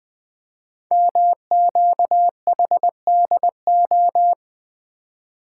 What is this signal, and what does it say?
Morse code, heard as a keyed tone, about 20 wpm, 700 Hz. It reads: MQHDO
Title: Code Groups - 5 Letters - 20wpm